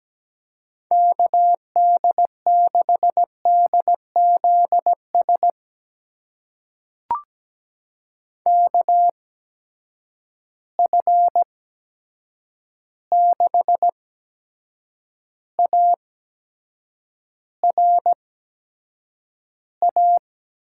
Morse code, heard as a keyed tone, about 17 wpm, spaced as for 7 wpm, 700 Hz. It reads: KD6DZS KF6ARA
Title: USA Call Signs - 6 Characters - 8x Character Spacing - 17wpm